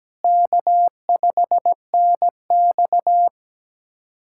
Morse code, heard as a keyed tone, about 17 wpm, 700 Hz. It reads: K5NX
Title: USA Callsign DE Callsign - 4 Characters - 17wpm